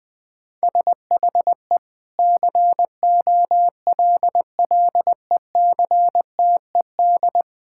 Morse code, heard as a keyed tone, about 20 wpm, 700 Hz. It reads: SHE COLLECTED
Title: Sentences with 2-Word Phrases - 20wpm